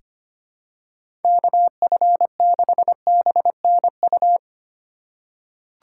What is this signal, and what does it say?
Morse code, heard as a keyed tone, about 25 wpm, 700 Hz. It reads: KF6BNU
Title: USA Call Signs - 6 Characters - 3x Character Spacing - 25wpm